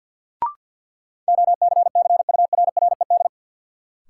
Morse code, heard as a keyed tone, about 51 wpm, 700 Hz. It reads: OCCURRED
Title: Sets of 5 Words Encore - New Word Drill - 50wpm